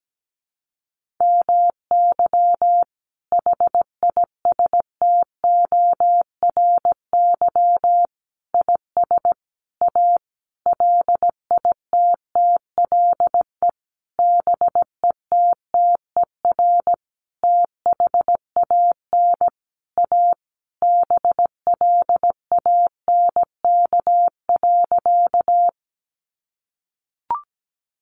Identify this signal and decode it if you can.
Morse code, heard as a keyed tone, about 17 wpm, 700 Hz. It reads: MY HISTORY IS A LITTLE BETTER THAN A BLANK.
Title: Sentences from Top 1500 Words 17wpm